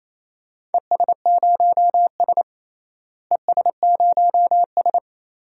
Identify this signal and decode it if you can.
Morse code, heard as a keyed tone, about 28 wpm, 700 Hz. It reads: ES0S ES0S
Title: DXCC-100 Call Signs - 4 Characters - Repeated 3x - 28wpm